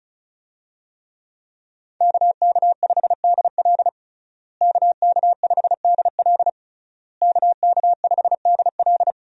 Morse code, heard as a keyed tone, about 36 wpm, 700 Hz. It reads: KK5DL KK5DL KK5DL
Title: USA Call Signs - 5 Characters - Repeated 3x - 35wpm